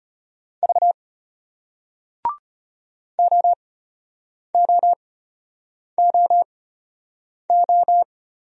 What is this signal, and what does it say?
Morse code, heard as a keyed tone, about 34 wpm, spaced as for 8 wpm, 700 Hz. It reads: V OOOO